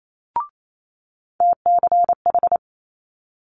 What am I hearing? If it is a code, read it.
Morse code, heard as a keyed tone, about 28 wpm, 700 Hz. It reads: TCH
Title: Multi-letter Phonemes Quick Fire 28wpm